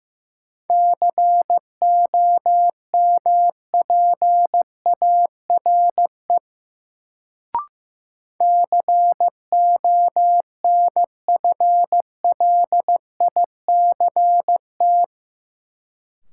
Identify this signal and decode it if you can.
Morse code, heard as a keyed tone, about 15 wpm, 700 Hz. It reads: COMPARE CONFLICT
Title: Sentences from Top 1300 Words - New Word Drill - 15wpm